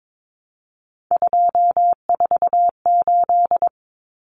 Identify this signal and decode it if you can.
Morse code, heard as a keyed tone, about 22 wpm, 700 Hz. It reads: 248
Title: Numbers - 3-Digit 22wpm